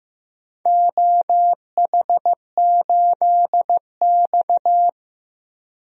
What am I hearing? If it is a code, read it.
Morse code, heard as a keyed tone, about 16 wpm, 700 Hz. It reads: OH8X
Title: DXCC-100 Call Signs - 4 Characters - Repeated 2x - 15wpm